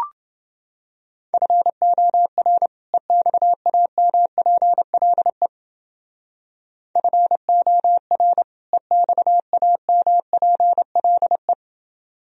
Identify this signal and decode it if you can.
Morse code, heard as a keyed tone, about 29 wpm, 700 Hz. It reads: FOR EXAMPLE FOR EXAMPLE